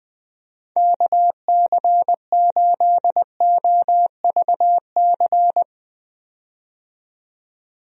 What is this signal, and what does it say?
Morse code, heard as a keyed tone, about 20 wpm, 700 Hz. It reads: KC8OVC
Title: USA Call Signs - 6 Characters - 20wpm